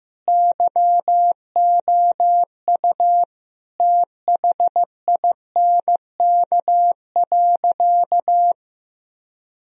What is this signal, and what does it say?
Morse code, heard as a keyed tone, about 15 wpm, 700 Hz. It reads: YOU THINK.
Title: Sentences from Top 1000 Words - Review Part 3 - 15wpm